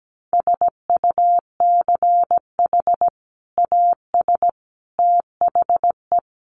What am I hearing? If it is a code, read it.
Morse code, heard as a keyed tone, about 17 wpm, 700 Hz. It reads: SUCH AS THE
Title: Sets of 3 Words 17wpm